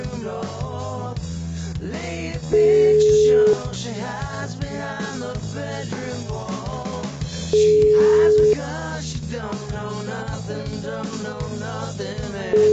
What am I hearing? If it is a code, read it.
Ringback tone (Europe, 425 Hz).